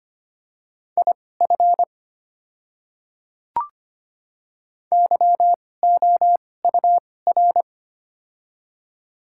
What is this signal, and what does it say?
Morse code, heard as a keyed tone, about 25 wpm, spaced as for 18 wpm, 700 Hz. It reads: IF YOUR
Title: 2x Character Spacing - Top 300 Words - 25wpm